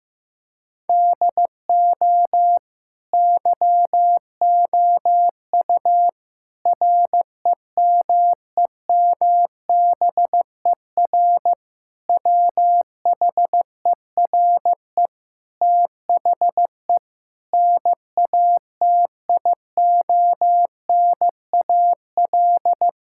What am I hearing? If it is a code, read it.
Morse code, heard as a keyed tone, about 15 wpm, 700 Hz. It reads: DO YOU REMEMBER WHERE THE NATIONAL
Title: Sentences from Top 1000 Words - Review Part 6 - 15wpm